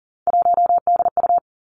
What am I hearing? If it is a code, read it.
Morse code, heard as a keyed tone, about 37 wpm, 700 Hz. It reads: 1DU
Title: Three Challenging Letter-Number Combinations 40wpm